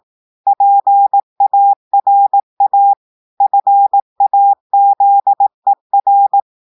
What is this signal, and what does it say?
Morse code, heard as a keyed tone, about 18 wpm, 800 Hz. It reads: PARA FAZER
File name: audios / portugais / portugais texte en clair 2 vitesse 18 mots minutes